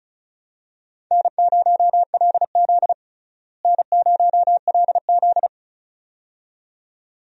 Morse code, heard as a keyed tone, about 36 wpm, 700 Hz. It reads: N0LZ N0LZ
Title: USA Call Signs - 4 Characters - Repeated 2x - 35wpm